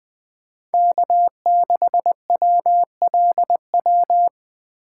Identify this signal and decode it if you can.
Morse code, heard as a keyed tone, about 20 wpm, 700 Hz. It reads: K6WLW
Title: Speed Racing - USA Call Signs 17wpm